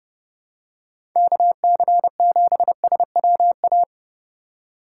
Morse code, heard as a keyed tone, about 31 wpm, 700 Hz. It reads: KC7SWA
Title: USA Call Signs - 6 Characters - 2x Character Spacing - 30wpm